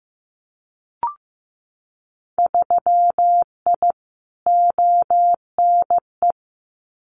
Morse code, heard as a keyed tone, about 15 wpm, 700 Hz. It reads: 3I ONE
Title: Winter Field Day Exchanges Encore 15wpm